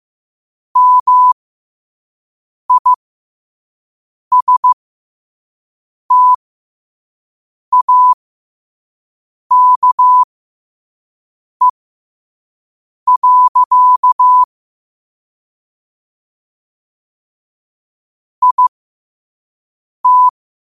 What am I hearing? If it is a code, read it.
Morse code, heard as a keyed tone, about 15 wpm, spaced as for 5 wpm, 1000 Hz. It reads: MISTAKE. IT
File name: qotdfiles / 5wpm